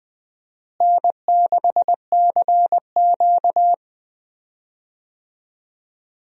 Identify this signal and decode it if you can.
Morse code, heard as a keyed tone, about 20 wpm, 700 Hz. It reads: N6CQ
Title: USA Call Signs - 4 Characters - Repeated 2x - 20wpm